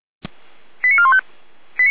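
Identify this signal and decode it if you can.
Five-tone selective call (ZVEI), heard as quick runs of five tones, tones of about 80 ms. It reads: 89415
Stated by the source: ZVEI five-tone selective call received on an SDR.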